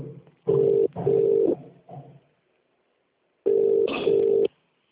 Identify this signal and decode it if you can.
Ringback tone (UK), 400 + 450 Hz.